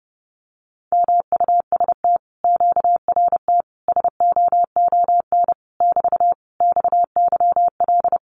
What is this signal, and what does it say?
Morse code, heard as a keyed tone, about 30 wpm, 700 Hz. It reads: MUST QRT SOON <BT> XYL
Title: Common QSO Phrases Encore 30wpm